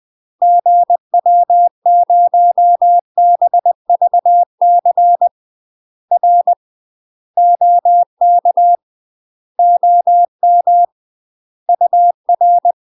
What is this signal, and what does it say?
Morse code, heard as a keyed tone, about 20 wpm, 700 Hz. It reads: GW0BVC R OK OM UR
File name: MASTER QSOs TRAINING/20WPM